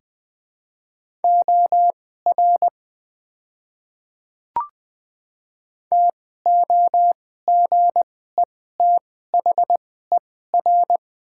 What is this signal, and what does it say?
Morse code, heard as a keyed tone, about 20 wpm, spaced as for 15 wpm, 700 Hz. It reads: OR TOGETHER
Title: 2x Character Spacing - Top 300 Words - 20wpm